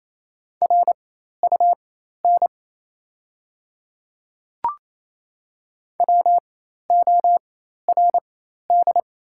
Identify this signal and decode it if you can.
Morse code, heard as a keyed tone, about 28 wpm, spaced as for 13 wpm, 690 Hz. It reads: RUN WORD
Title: 4x Character Spacing - Top 500 Words - 28wpm